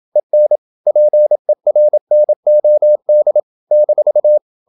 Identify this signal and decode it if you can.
Morse code, heard as a keyed tone, about 27 wpm, 600 Hz. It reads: EN PERNOD -
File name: Ferie QRQ 27wpm